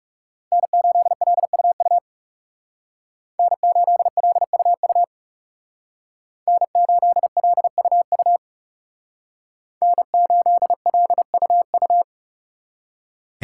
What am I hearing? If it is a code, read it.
Morse code, heard as a keyed tone, about 37 wpm, 700 Hz. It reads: N8LUU N8LUU N8LUU N8LUU
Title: Speed Racing - USA 1x3 Call Signs 30wpm